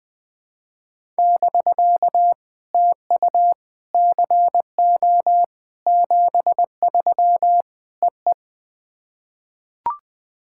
Morse code, heard as a keyed tone, about 20 wpm, 660 Hz. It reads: <BK> TU CO 73 EE
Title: POTA QSO Practice - 20wpm